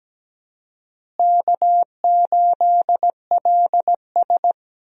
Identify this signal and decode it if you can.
Morse code, heard as a keyed tone, about 17 wpm, 700 Hz. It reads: K8LS
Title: USA Call Signs - 4 Characters - Repeated 2x - 17wpm